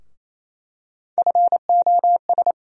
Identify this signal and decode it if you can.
Morse code, heard as a keyed tone, about 28 wpm, 700 Hz. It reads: FOS
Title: USA Call Sign Postfixes - 3 Letters 28wpm